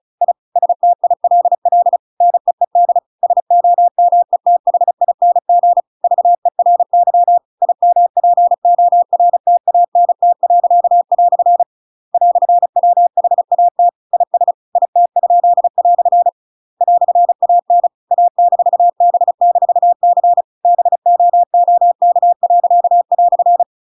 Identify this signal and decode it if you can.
Morse code, heard as a keyed tone, about 34 wpm, 700 Hz. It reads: I STILL NEED SOMETHING VERY IMPORTANT." "WHAT IS IT?" "AN A-B-C BOOK."
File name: Pinocchio / Pinocchio ch8 35 35 ik1hge